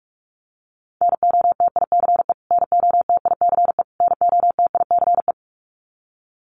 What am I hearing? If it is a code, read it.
Morse code, heard as a keyed tone, about 46 wpm, 700 Hz. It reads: NOTICE NOTICE NOTICE
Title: Top 500 Words Repeated 45wpm